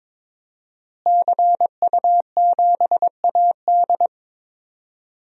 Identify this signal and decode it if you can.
Morse code, heard as a keyed tone, about 22 wpm, 700 Hz. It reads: CU7AD
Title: DXCC-100 Call Signs - 5 Characters - Repeated 3x - 22wpm